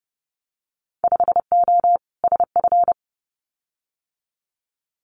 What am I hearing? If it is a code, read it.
Morse code, heard as a keyed tone, about 30 wpm, 700 Hz. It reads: 5O SF
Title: Winter Field Day Exchanges Encore 30wpm